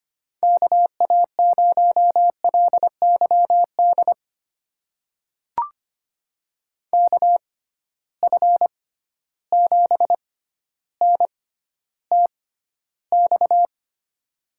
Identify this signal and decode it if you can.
Morse code, heard as a keyed tone, about 25 wpm, spaced as for 13 wpm, 700 Hz. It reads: KA0LYD KF7NTX
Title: USA Call Signs - 6 Characters - 6x Character Spacing - 25wpm